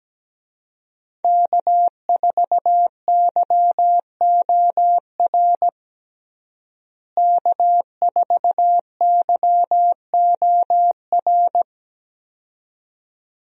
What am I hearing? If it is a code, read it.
Morse code, heard as a keyed tone, about 17 wpm, 700 Hz. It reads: K4YOR K4YOR
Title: USA Call Signs - 5 Characters - Repeated 2x - 17wpm